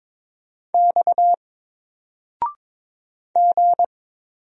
Morse code, heard as a keyed tone, about 23 wpm, 700 Hz. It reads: X G
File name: Single-Letter-Number-22wpm